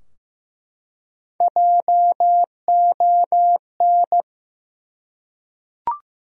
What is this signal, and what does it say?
Morse code, heard as a keyed tone, about 15 wpm, 700 Hz. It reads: JON
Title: USA Common Names - Max 3 Letters - 15wpm